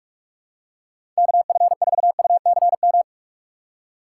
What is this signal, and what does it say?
Morse code, heard as a keyed tone, about 46 wpm, 700 Hz. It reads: KF4UCM
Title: USA Call Signs - 6x Character Spacing - 45wpm